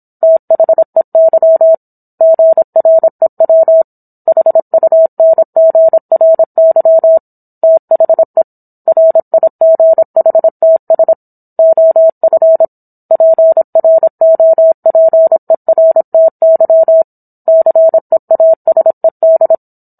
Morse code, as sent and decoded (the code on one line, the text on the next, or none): - .... -.-- / --. .-. .-- / .... ..- -. --. .-. -.-- / - .... / .-. .. --. .... - ... / --- ..-. / .--. .-. --- .--. .-. - -.-- / -.-. .- ... -..
THEY GREW HUNGRY THE RIGHTS OF PROPERTY CEASED